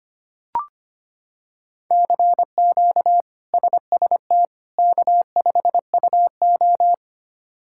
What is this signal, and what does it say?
Morse code, heard as a keyed tone, about 25 wpm, 700 Hz. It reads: CQ SST K5UO
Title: K1USN SST QSO Practice - 25wpm